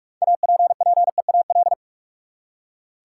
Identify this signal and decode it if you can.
Morse code, heard as a keyed tone, about 46 wpm, 700 Hz. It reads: APPEAL
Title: Six Letter Words - 45wpm